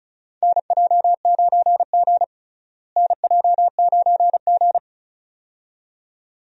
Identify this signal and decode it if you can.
Morse code, heard as a keyed tone, about 36 wpm, 700 Hz. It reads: NJ9G NJ9G
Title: USA Call Signs - 4 Characters - Repeated 2x - 35wpm